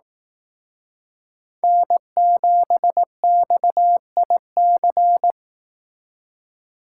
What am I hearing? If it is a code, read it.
Morse code, heard as a keyed tone, about 18 wpm, 700 Hz. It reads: N7XIC
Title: Speed Racing - USA 1x3 Call Signs 15wpm